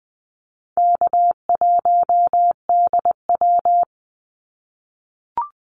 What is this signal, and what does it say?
Morse code, heard as a keyed tone, about 20 wpm, 700 Hz. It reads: K1DW
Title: Easy Call Signs - 20wpm